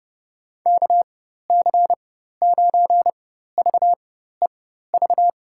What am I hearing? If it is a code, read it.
Morse code, heard as a keyed tone, about 30 wpm, spaced as for 14 wpm, 700 Hz. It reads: KC9VEV
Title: USA Call Signs - 6 Characters - 4x Character Spacing - 30wpm